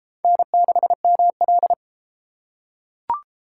Morse code, heard as a keyed tone, about 30 wpm, 700 Hz. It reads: N6ML